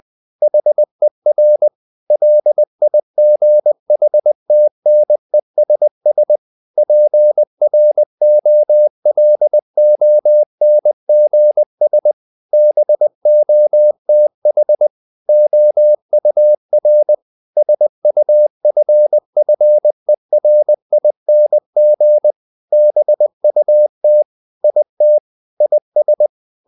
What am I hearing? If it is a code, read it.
Morse code, heard as a keyed tone, about 20 wpm, 600 Hz. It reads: HER LIGHTNESS PROLONGS BOTH OUR SUFFERING BUT IT IS
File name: the old man and the sea part 2 20wpm